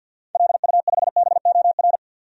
Morse code, heard as a keyed tone, about 53 wpm, 700 Hz. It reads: LU5BOR
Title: DXCC-100 Call Signs - 6 Characters - 50wpm